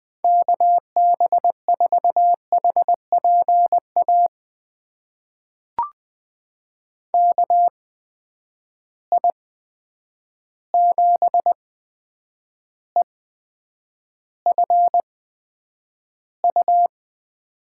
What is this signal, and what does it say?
Morse code, heard as a keyed tone, about 20 wpm, spaced as for 9 wpm, 700 Hz. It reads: KB4HPA KI7EFU
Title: USA Call Signs - 6 Characters - 8x Character Spacing - 20wpm